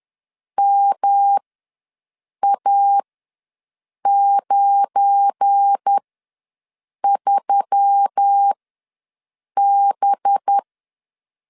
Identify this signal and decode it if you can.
Morse code, heard as a keyed tone, about 10 wpm, spaced as for 6 wpm, 800 Hz. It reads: MA93B
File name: LEZIONI / 7 GF9Z / 7 GF9 ws 10wpm 4034